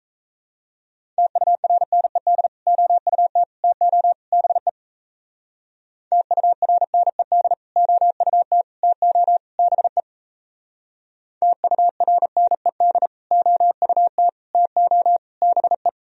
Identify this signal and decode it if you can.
Morse code, heard as a keyed tone, about 34 wpm, 700 Hz. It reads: TURNED OUT TO BE TURNED OUT TO BE TURNED OUT TO BE
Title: Speed-Racing - Sets of 4 Words 28wpm